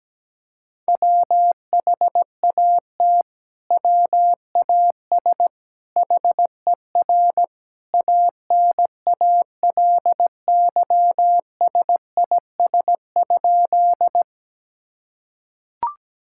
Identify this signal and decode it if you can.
Morse code, heard as a keyed tone, about 17 wpm, 700 Hz. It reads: WHAT WAS HER ANALYSIS?
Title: Sentences from Top 1000 Words - Review Part 5 - 17wpm